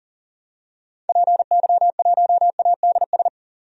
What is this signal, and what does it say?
Morse code, heard as a keyed tone, about 40 wpm, 700 Hz. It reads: PY1ADS
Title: DXCC-100 Call Signs - 6 Characters - 40wpm